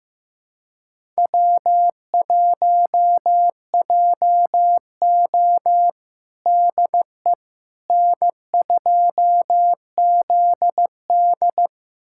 Morse code, heard as a keyed tone, about 15 wpm, 700 Hz. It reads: W1JO DE N2ZD
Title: USA Callsign DE Callsign - 4 Characters - 15wpm